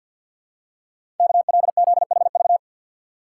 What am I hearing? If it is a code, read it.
Morse code, heard as a keyed tone, about 52 wpm, 700 Hz. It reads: KLZHV
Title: Code Groups - 5 Letters - 50wpm